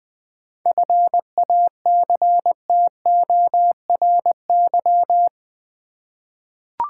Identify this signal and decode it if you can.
Morse code, heard as a keyed tone, about 20 wpm, 700 Hz. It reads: FACTORY
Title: 7 Letter Words - First 3 Letters Given - 20wpm